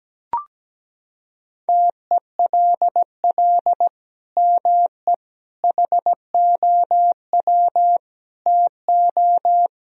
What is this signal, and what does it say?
Morse code, heard as a keyed tone, about 17 wpm, 700 Hz. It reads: TELL ME HOW TO
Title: Sentences from Top 1000 Words - Review Part 2 - 17wpm